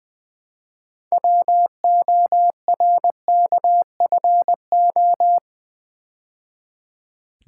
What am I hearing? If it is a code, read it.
Morse code, heard as a keyed tone, about 20 wpm, 700 Hz. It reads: WORKFO
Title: Build A Word - 3 Letters At A Time - 9-Letter Words - 20wpm